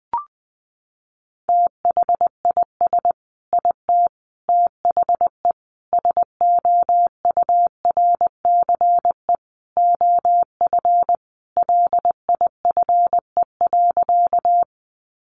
Morse code, heard as a keyed tone, about 20 wpm, 700 Hz. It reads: THIS IT THE SOURCE OF LIFE.
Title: Sentences from Top 800 Words 20wpm